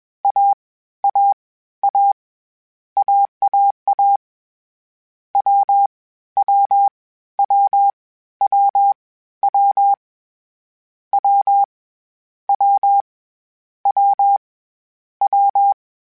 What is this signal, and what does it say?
Morse code, heard as a keyed tone, about 21 wpm, 800 Hz. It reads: A A A AAA W W W W W W W W W
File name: Difficult Combos / Difficult Combos-mp3s / AWJ1 @20